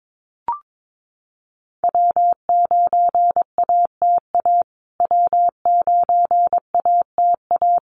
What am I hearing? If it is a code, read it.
Morse code, heard as a keyed tone, about 22 wpm, 700 Hz. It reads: W9ATA W9ATA
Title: USA Call Signs Repeated 3x 22wpm